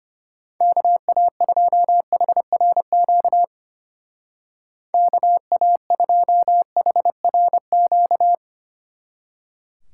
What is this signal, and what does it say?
Morse code, heard as a keyed tone, about 31 wpm, 700 Hz. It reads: KA2HRQ KA2HRQ